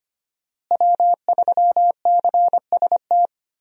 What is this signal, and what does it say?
Morse code, heard as a keyed tone, about 25 wpm, 700 Hz. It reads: W3CST
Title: USA Call Signs - 5 Characters - Repeated 2x - 25wpm